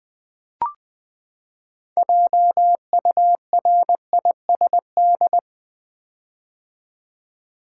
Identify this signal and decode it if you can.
Morse code, heard as a keyed tone, about 20 wpm, 700 Hz. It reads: JURISD
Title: Build A Word - 6 Letters At A Time - 12-Letter Words - 20wpm